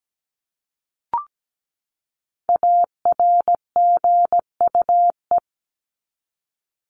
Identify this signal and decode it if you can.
Morse code, heard as a keyed tone, about 17 wpm, 700 Hz. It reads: ARGUE
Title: Top 1501-1750 Words 17wpm